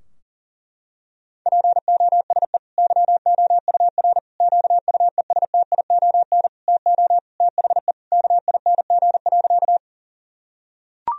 Morse code, heard as a keyed tone, about 40 wpm, 700 Hz. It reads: POSE YOUR QUESTION TO THE KING.
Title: Sentences from Top 1000 Words - Review Part 2 - 40wpm